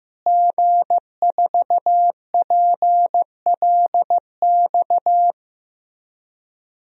Morse code, heard as a keyed tone, about 15 wpm, 700 Hz. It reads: G4PLX
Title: DXCC-100 Call Signs - 5 Characters - Repeated 3x - 15wpm